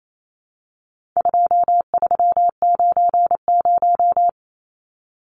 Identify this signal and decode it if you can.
Morse code, heard as a keyed tone, about 28 wpm, 700 Hz. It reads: 2390
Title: Numbers - 4-Digit 28wpm